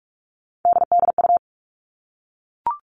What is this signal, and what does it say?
Morse code, heard as a keyed tone, about 44 wpm, 700 Hz. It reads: DDU
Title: All Cut Numbers - 3-Digit 45wpm